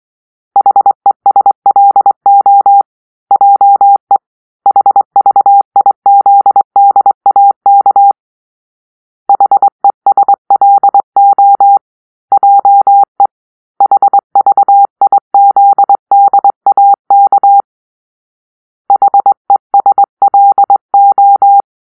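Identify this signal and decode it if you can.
Morse code, heard as a keyed tone, about 22 wpm, 800 Hz. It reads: HESLO JE HVIZDAK HESLO JE HVIZDAK HESLO
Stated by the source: Morse-code cipher made by a player team for the Kachna puzzle game.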